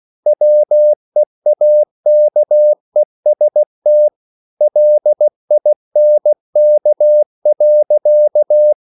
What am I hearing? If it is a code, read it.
Morse code, heard as a keyed tone, about 16 wpm, 600 Hz. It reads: WEAKEST LINK.
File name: CW / Practice / EnglishSayings16